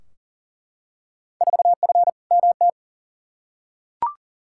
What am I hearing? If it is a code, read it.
Morse code, heard as a keyed tone, about 40 wpm, 700 Hz. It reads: 4F MT
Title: Field Day Exchanges Encore - 40wpm